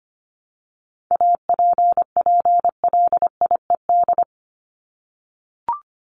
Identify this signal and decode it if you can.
Morse code, heard as a keyed tone, about 25 wpm, 700 Hz. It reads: APPLIED
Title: ED Suffix - Top 100 Words with Base Words 25wpm